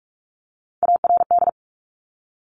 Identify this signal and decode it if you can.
Morse code, heard as a keyed tone, about 46 wpm, 690 Hz. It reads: ARD
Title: Top Three Letter Combinations in Words 45wpm